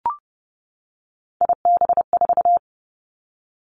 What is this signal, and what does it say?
Morse code, heard as a keyed tone, about 30 wpm, 700 Hz. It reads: IB4
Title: Three Challenging Letter-Number Combinations 30wpm